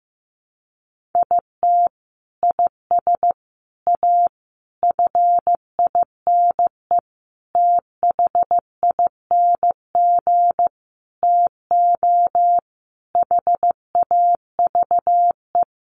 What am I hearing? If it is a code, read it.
Morse code, heard as a keyed tone, about 15 wpm, 700 Hz. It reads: IT IS A FINE THING TO HAVE
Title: Sentences from Top 1000 Words 15wpm